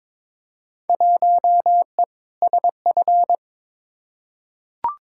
Morse code, heard as a keyed tone, about 22 wpm, 700 Hz. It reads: 1E SF
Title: Field Day Exchanges - 22wpm